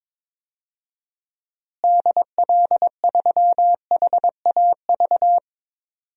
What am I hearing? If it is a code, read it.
Morse code, heard as a keyed tone, about 22 wpm, 700 Hz. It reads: DL3HAV
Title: DXCC-100 Call Signs - 6 Characters - Repeated 2x - 22wpm